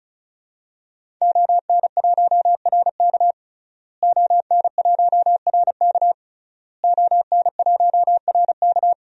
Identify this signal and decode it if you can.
Morse code, heard as a keyed tone, about 36 wpm, 700 Hz. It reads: ON1RK ON1RK ON1RK
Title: DXCC-100 Call Signs - 5 Characters - Repeated 3x - 35wpm